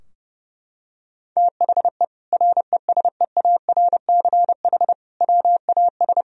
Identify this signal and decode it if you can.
Morse code, heard as a keyed tone, about 30 wpm, 700 Hz. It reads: THE RESEARCH WAS
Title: Sentences from Top 1000 Words - Review Part 4 - 30wpm